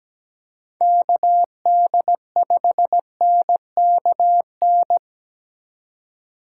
Morse code, heard as a keyed tone, about 17 wpm, 700 Hz. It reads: KD5NKN
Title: USA Call Signs - 6 Characters - 3x Character Spacing - 17wpm